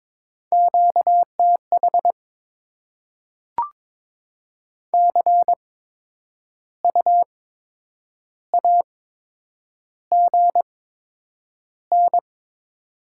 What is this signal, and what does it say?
Morse code, heard as a keyed tone, about 22 wpm, spaced as for 8 wpm, 700 Hz. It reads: QTH CU AGN
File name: Top 20 QSO Elements - 8x Character Spacing - 22wpm